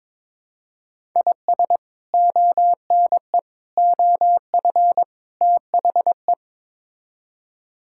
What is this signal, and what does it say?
Morse code, heard as a keyed tone, about 22 wpm, 700 Hz. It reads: IS ONE OF THE
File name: practice / 22wpm / Build A Phrase - 5-Word Phrases - 22wpm